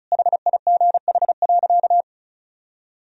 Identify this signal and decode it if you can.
Morse code, heard as a keyed tone, about 36 wpm, 690 Hz. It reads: HIGH.
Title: Sentences from Top 1000 Words - Review Part 1 - 35wpm